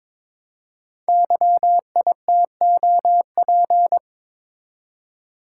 Morse code, heard as a keyed tone, about 22 wpm, 700 Hz. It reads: YITOP
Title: Code Groups - 5 Letters - 22wpm